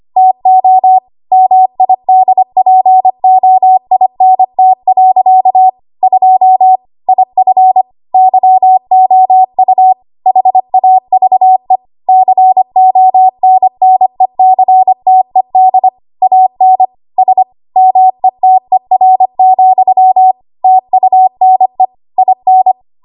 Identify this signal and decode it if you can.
Morse code, heard as a keyed tone, about 25 wpm, 750 Hz. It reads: TO MIDPOINT. 2 IF YOU HAVE CONNECTED AN S METER, TUNE IN
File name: Archive / 25 WPM / 25WPM